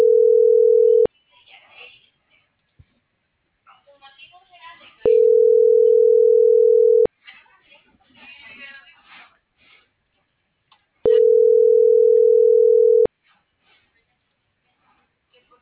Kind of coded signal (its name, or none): telephone call progress tone